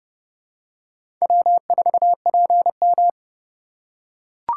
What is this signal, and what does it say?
Morse code, heard as a keyed tone, about 30 wpm, 700 Hz. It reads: W4PM
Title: USA Call Signs - 4 Characters - 2x Character Spacing - 30wpm